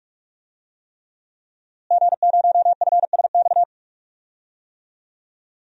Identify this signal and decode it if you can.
Morse code, heard as a keyed tone, about 42 wpm, 700 Hz. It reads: G0FSX
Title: DXCC-100 Call Signs - 5 Characters - Repeated 3x - 45wpm